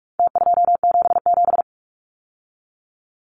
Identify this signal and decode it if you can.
Morse code, heard as a keyed tone, about 44 wpm, 700 Hz. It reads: T277
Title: Numbers - Contest Style - 4-Digit 45wpm